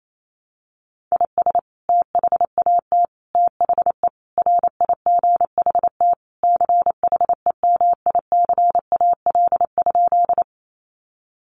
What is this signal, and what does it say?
Morse code, heard as a keyed tone, about 28 wpm, 700 Hz. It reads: IS THAT THE RIGHT CHEMICAL?
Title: Sentences from Top 1200 Words 28wpm